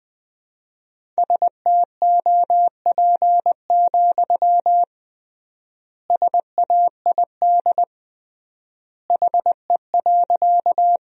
Morse code, heard as a keyed tone, about 20 wpm, 700 Hz. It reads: STOP, SAID HE.
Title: 3x Word-Spacing - Sentences from Top 300 Words - 20wpm